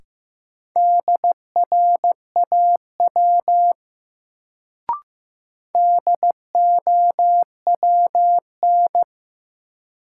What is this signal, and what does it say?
Morse code, heard as a keyed tone, about 15 wpm, 700 Hz. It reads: DRAW DOWN
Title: Sentences from Top 300 Words - New Word Drill - 15wpm